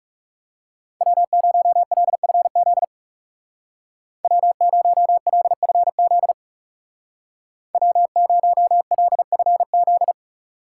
Morse code, heard as a keyed tone, about 37 wpm, 700 Hz. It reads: W0LFZ W0LFZ W0LFZ
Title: Speed Racing - USA 1x3 Call Signs 30wpm